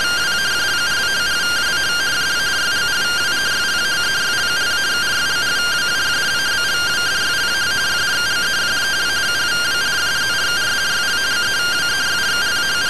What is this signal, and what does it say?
Two-tone teleprinter signal, about 170 Hz shift.